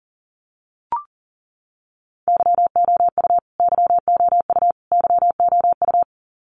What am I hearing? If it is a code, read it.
Morse code, heard as a keyed tone, about 40 wpm, 700 Hz. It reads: YOU YOU YOU
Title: Top 300 Words Repeated 40wpm